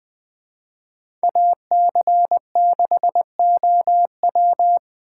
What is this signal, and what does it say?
Morse code, heard as a keyed tone, about 20 wpm, 700 Hz. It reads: AC6OW
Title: Speed Racing - USA Call Signs 17wpm